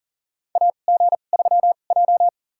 Morse code, heard as a keyed tone, about 40 wpm, spaced as for 29 wpm, 700 Hz. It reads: AG3J
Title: USA Call Signs - 4 Characters - 2x Character Spacing - 40wpm